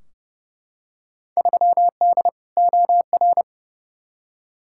Morse code, heard as a keyed tone, about 31 wpm, 700 Hz. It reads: 3D OR
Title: Field Day Exchanges - 30wpm